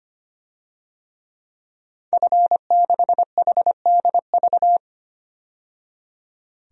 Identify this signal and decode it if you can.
Morse code, heard as a keyed tone, about 25 wpm, 700 Hz. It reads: F6HDV
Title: DXCC-100 Call Signs - 5 Characters - 25wpm